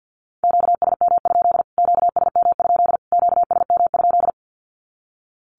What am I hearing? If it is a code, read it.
Morse code, heard as a keyed tone, about 50 wpm, 700 Hz. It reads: QSM? QSM? QSM?
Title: Q-Codes Repeated 3x 50wpm